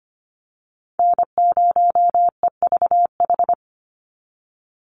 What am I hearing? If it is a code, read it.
Morse code, heard as a keyed tone, about 25 wpm, 700 Hz. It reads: N0EVH
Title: NA SOTA Chaser Callsigns 25wpm